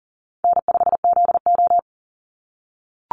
Morse code, heard as a keyed tone, about 39 wpm, 700 Hz. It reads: N5ZO
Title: Call Signs - 40wpm